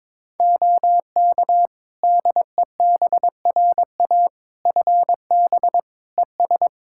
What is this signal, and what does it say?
Morse code, heard as a keyed tone, about 22 wpm, 700 Hz. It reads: OK DEBRA FB ES
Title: Practice QSO - Exchange 2 - Info - 22wpm